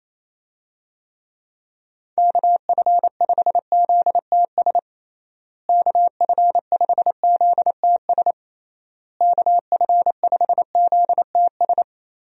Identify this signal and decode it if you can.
Morse code, heard as a keyed tone, about 28 wpm, 700 Hz. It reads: KF5ZTS KF5ZTS KF5ZTS
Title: USA Call Signs - 6 Characters - Repeated 3x - 28wpm